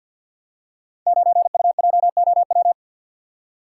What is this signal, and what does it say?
Morse code, heard as a keyed tone, about 50 wpm, 700 Hz. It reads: 9UJOW
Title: Code Groups - 5 Letter:Numbers - 50wpm